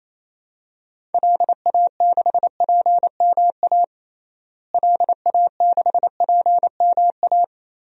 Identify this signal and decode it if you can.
Morse code, heard as a keyed tone, about 28 wpm, 700 Hz. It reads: LA6PMA LA6PMA
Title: DXCC-100 Call Signs - 6 Characters - Repeated 2x - 28wpm